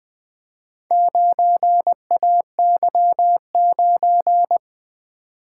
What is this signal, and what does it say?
Morse code, heard as a keyed tone, about 20 wpm, 700 Hz. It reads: 9AY9
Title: Four Letter-Number Combinations - 20wpm